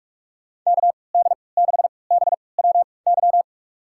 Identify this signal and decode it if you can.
Morse code, heard as a keyed tone, about 46 wpm, spaced as for 26 wpm, 700 Hz. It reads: KD6BWY
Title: USA Call Signs - 3x Character Spacing - 45wpm